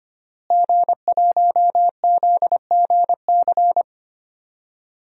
Morse code, heard as a keyed tone, about 25 wpm, 700 Hz. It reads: G1ZGC
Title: DXCC-100 Call Signs - 5 Characters - Repeated 3x - 25wpm